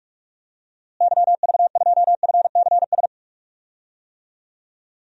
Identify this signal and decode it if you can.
Morse code, heard as a keyed tone, about 46 wpm, 700 Hz. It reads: YV2FCS
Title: DXCC-100 Call Signs - 6 Characters - Repeated 2x - 45wpm